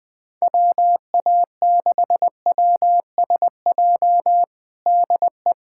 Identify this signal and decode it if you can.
Morse code, heard as a keyed tone, about 20 wpm, 700 Hz. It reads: WA6WSJ DE
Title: USA Callsign DE Callsign - 6 Characters - 20wpm